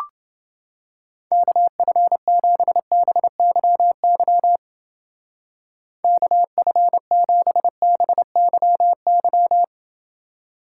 Morse code, heard as a keyed tone, about 30 wpm, 700 Hz. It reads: KF7BYY KF7BYY